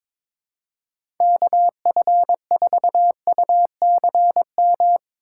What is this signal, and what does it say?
Morse code, heard as a keyed tone, about 22 wpm, 700 Hz. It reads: KF4UCM
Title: USA Call Signs - 6x Character Spacing - 22wpm